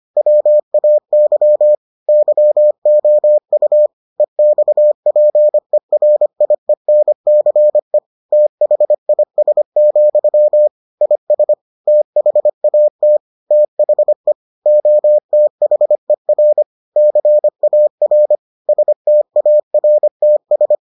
Morse code, heard as a keyed tone, about 25 wpm, 600 Hz. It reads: WAY YOU EXPERIENCE THIS, IS THAT THE OTHER CAR STARTS